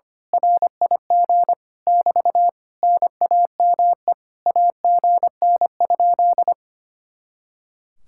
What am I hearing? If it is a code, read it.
Morse code, heard as a keyed tone, about 24 wpm, 700 Hz. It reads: RIG <BT> NAME AGN?
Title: Speed-Racing Common QSO Phrases Encore - 25wpm